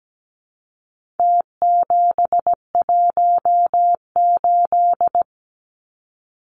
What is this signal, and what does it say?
Morse code, heard as a keyed tone, about 17 wpm, 700 Hz. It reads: T718
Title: Numbers - Contest Style - 4-Digit 17wpm